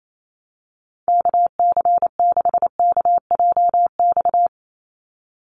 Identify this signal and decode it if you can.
Morse code, heard as a keyed tone, about 28 wpm, 700 Hz. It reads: KC6KJX
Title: USA Call Signs 28wpm